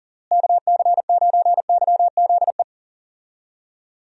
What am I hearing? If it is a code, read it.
Morse code, heard as a keyed tone, about 40 wpm, 700 Hz. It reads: KC9YZE
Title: USA Call Signs - 6 Characters - Repeated 3x - 40wpm